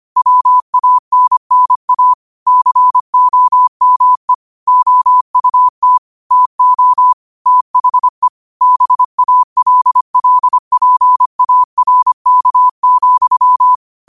Morse code, heard as a keyed tone, about 25 wpm, 1000 Hz. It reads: WANNA COME OUT TO THE BALLPARK,
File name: qotdfiles / 25wpm